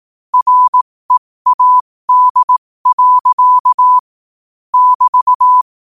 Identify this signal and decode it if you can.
Morse code, heard as a keyed tone, about 18 wpm, spaced as for 15 wpm, 1000 Hz. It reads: READ. <BT>
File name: qotdfiles / 15wpm